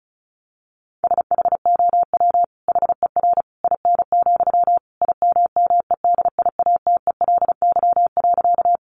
Morse code, heard as a keyed tone, about 34 wpm, 700 Hz. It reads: SHOW HER IN, IMMEDIATELY.
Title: Sentences from Top 1100 Words 35wpm